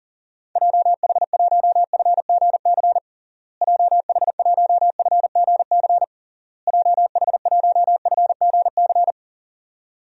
Morse code, heard as a keyed tone, about 40 wpm, 700 Hz. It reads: JH1FGC JH1FGC JH1FGC
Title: DXCC-100 Call Signs - 6 Characters - Repeated 3x - 40wpm